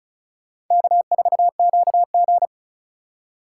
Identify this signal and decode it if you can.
Morse code, heard as a keyed tone, about 45 wpm, 700 Hz. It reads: K4QG